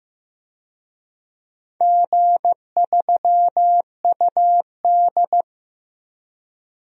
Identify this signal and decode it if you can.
Morse code, heard as a keyed tone, about 15 wpm, 700 Hz. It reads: G3UD
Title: DXCC-100 Call Signs - 4 Characters - 15wpm